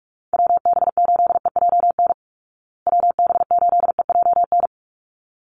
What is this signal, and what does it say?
Morse code, heard as a keyed tone, about 44 wpm, 700 Hz. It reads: WB8EJN WB8EJN
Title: Call Signs Repeated 3x - 45wpm